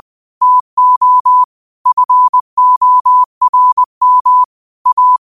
Morse code, heard as a keyed tone, about 20 wpm, 1000 Hz. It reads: TO FORM A
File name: qotdfiles / 20wpm